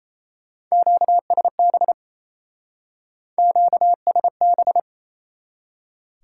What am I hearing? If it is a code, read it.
Morse code, heard as a keyed tone, about 36 wpm, 700 Hz. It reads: QSB QSB